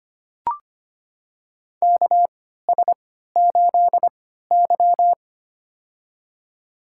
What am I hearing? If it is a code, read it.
Morse code, heard as a keyed tone, about 25 wpm, spaced as for 14 wpm, 700 Hz. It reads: KS8Y
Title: USA Call Signs - 4 Characters - 3x Character Spacing - 25wpm